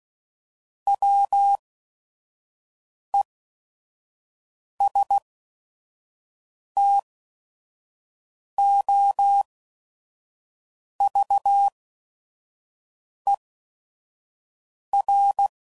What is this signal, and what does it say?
Morse code, heard as a keyed tone, about 16 wpm, spaced as for 5 wpm, 800 Hz. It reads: WESTOVER
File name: CW Code Groups @5WPM / fromSM's / Cities